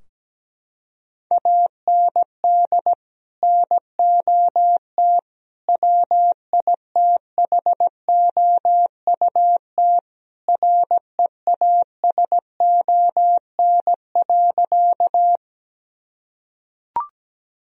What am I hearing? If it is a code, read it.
Morse code, heard as a keyed tone, about 17 wpm, 700 Hz. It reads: AND NOT WITHOUT REASON.
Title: Sentences from Top 1000 Words - Review Part 3 - 17wpm